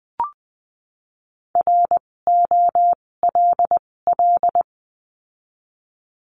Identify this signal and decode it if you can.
Morse code, heard as a keyed tone, about 20 wpm, spaced as for 16 wpm, 700 Hz. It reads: ROLL
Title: 256 Lesson 42a - Words 12-20wpm